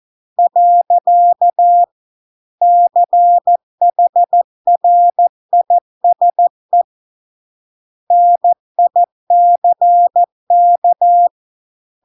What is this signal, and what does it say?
Morse code, heard as a keyed tone, about 14 wpm, 700 Hz. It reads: . CHRISE NICK